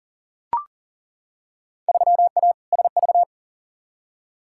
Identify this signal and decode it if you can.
Morse code, heard as a keyed tone, about 40 wpm, 700 Hz. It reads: 3A SV
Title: Field Day Exchanges - 40wpm